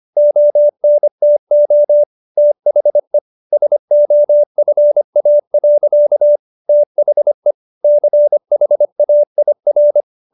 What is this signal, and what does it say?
Morse code, heard as a keyed tone, about 25 wpm, 590 Hz. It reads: ONTO THE SOFA. THE CHAIR